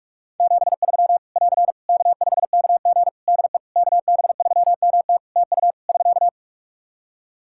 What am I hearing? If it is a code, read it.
Morse code, heard as a keyed tone, about 46 wpm, 700 Hz. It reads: 73 <AR> K5KG DE KB3MT TU <SK>
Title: K1USN SST QSO Practice 45wpm